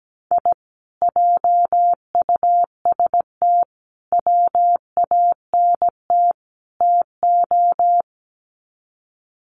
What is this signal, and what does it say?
Morse code, heard as a keyed tone, about 17 wpm, 700 Hz. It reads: I JUST WANT TO
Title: Sets of 4 Words 17wpm